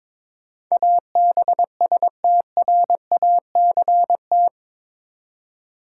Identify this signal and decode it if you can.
Morse code, heard as a keyed tone, about 22 wpm, 700 Hz. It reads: ABSTRACT
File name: practice / 22wpm / Eight Letter Words - 22wpm